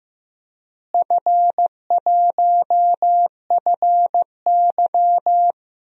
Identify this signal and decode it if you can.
Morse code, heard as a keyed tone, about 16 wpm, 700 Hz. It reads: F1FY
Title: DXCC-100 Call Signs - 4 Characters - Repeated 3x - 15wpm